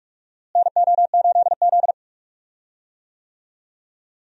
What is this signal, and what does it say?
Morse code, heard as a keyed tone, about 46 wpm, 700 Hz. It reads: NO8Z
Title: USA Call Signs - 4 Characters - Repeated 2x - 45wpm